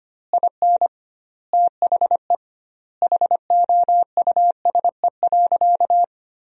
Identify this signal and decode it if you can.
Morse code, heard as a keyed tone, about 25 wpm, 700 Hz. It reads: IN THE HOUSE.
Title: 2x Word-Spacing - Sentences from Top 500 Words - 25wpm